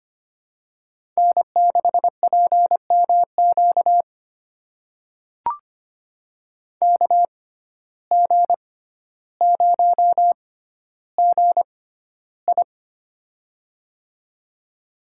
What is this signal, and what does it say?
Morse code, heard as a keyed tone, about 25 wpm, spaced as for 13 wpm, 700 Hz. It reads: N6PMQ KG0GI
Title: USA Call Signs - 5 Characters - 6x Character Spacing - 25wpm